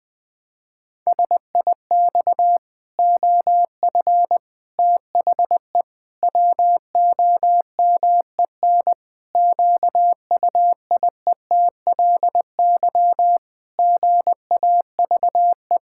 Morse code, heard as a keyed tone, about 20 wpm, 700 Hz. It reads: SIX OF THE WOMEN QUIETLY GAVE
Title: Pangram Challenge v2 - 20wpm